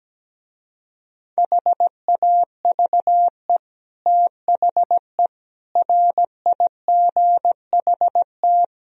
Morse code, heard as a keyed tone, about 17 wpm, 700 Hz. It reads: HAVE THE RIGHT
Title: Speed-Racing - Sets of 4 Words 17wpm